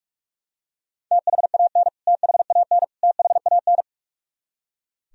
Morse code, heard as a keyed tone, about 44 wpm, 700 Hz. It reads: THAN THAN THAN
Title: Top 100 Words Repeated 45wpm